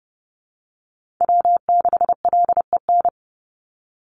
Morse code, heard as a keyed tone, about 30 wpm, 700 Hz. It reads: W6LEN
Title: Call Signs - 30wpm